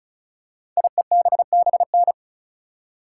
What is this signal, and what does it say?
Morse code, heard as a keyed tone, about 36 wpm, 690 Hz. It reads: IEBBN
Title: Code Groups - 5 Letters - 35wpm